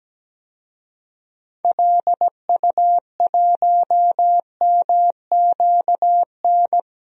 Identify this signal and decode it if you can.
Morse code, heard as a keyed tone, about 17 wpm, 700 Hz. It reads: LU1MQN
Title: DXCC-100 Call Signs - 6 Characters - 17wpm